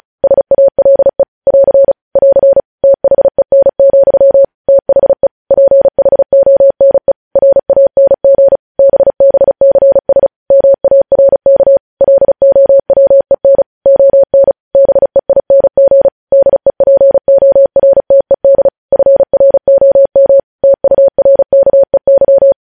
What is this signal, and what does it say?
Morse code, heard as a keyed tone, about 36 wpm, 550 Hz. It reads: SALE <AR> <AR> THEN, THE PHONE RANG BBCS MARK LOWEN ON BEING DEPORTED FROM TURKEY
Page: News headlines in Morse code at 35 wpm